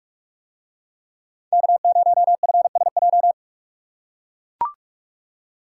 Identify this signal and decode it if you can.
Morse code, heard as a keyed tone, about 44 wpm, 700 Hz. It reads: K0FSJ